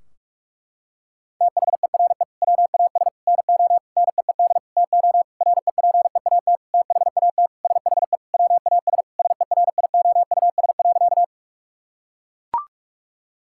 Morse code, heard as a keyed tone, about 41 wpm, 700 Hz. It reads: THERE WAS NO NEED TO REPEAT THAT SHE WAS SERIOUS.
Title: Sentences from Top 1600 Words 45wpm